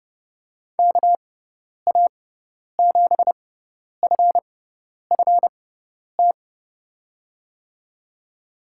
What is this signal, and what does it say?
Morse code, heard as a keyed tone, about 29 wpm, spaced as for 10 wpm, 700 Hz. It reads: KA7FFT
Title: USA Call Signs - 6 Characters - 6x Character Spacing - 30wpm